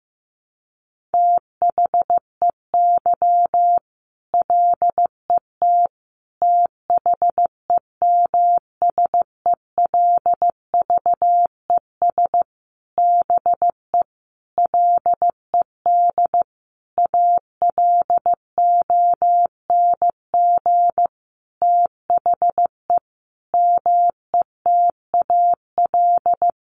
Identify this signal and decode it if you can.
Morse code, heard as a keyed tone, about 15 wpm, 700 Hz. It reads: THEY LET THEMSELVES BE LED ALONG THE METAL
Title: A Sound Of Thunder 15wpm